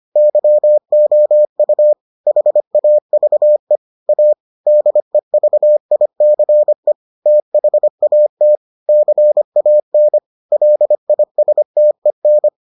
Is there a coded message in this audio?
Yes: Morse code from 0.2 to 12.5 s, heard as a keyed tone, about 25 wpm, 600 Hz. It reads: YOU HAVE A DEVICE THAT CAN LISTEN